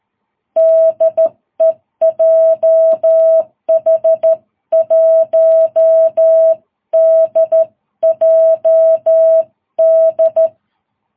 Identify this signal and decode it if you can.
Morse code, heard as a keyed tone, about 13 wpm, 650 Hz. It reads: DE JH1DJD
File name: jh8cbh / 2022cw